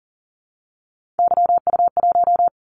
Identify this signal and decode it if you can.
Morse code, heard as a keyed tone, about 39 wpm, 700 Hz. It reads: YU1
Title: Callsign Prefixes for Top 25 DXCC - 40wpm